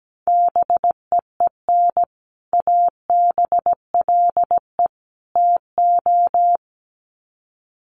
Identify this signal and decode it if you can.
Morse code, heard as a keyed tone, about 17 wpm, 700 Hz. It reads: BEEN ABLE TO
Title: Sets of 3 Words 17wpm